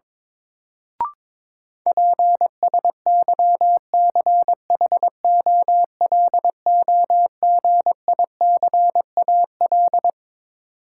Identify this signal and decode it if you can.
Morse code, heard as a keyed tone, about 22 wpm, 700 Hz. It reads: PSYCHOLOGICAL
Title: Sentences from Top 1400 Words - New Word Drill - 22wpm